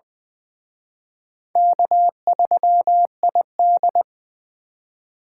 Morse code, heard as a keyed tone, about 24 wpm, 690 Hz. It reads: K3ID
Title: Speed Racing - USA 1x2 Call Signs 20wpm